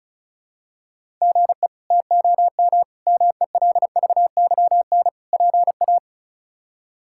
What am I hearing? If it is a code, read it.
Morse code, heard as a keyed tone, about 34 wpm, 700 Hz. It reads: GE TOM MELVYN PA
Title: K1USN SST QSO Practice - 35wpm